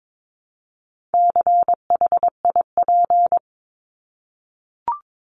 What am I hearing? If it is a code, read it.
Morse code, heard as a keyed tone, about 22 wpm, 700 Hz. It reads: CHIP
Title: Top 1751-2000 Words 22wpm